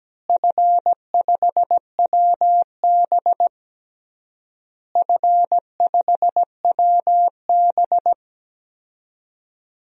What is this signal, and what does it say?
Morse code, heard as a keyed tone, about 17 wpm, 700 Hz. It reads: F5WB F5WB
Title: DXCC-100 Call Signs - 4 Characters - Repeated 3x - 17wpm